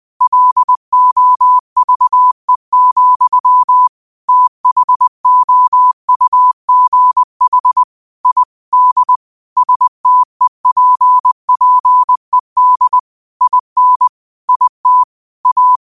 Morse code, heard as a keyed tone, about 20 wpm, 1000 Hz. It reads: LOVE, THOUGH I D STEPPED IN IT A